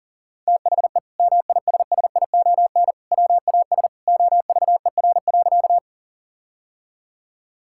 Morse code, heard as a keyed tone, about 37 wpm, 700 Hz. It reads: THE MISSION WAS OVER.
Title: Sentences from Top 1600 Words 40wpm